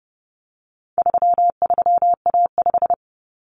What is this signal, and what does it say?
Morse code, heard as a keyed tone, about 30 wpm, 700 Hz. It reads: 33A5
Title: Common Cut Numbers - 4-Digit 30wpm